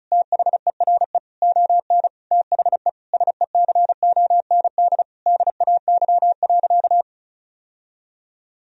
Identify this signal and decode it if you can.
Morse code, heard as a keyed tone, about 34 wpm, 700 Hz. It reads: THERE ON THE SECOND DAY.
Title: Sentences from Top 1000 Words - Review Part 4 - 35wpm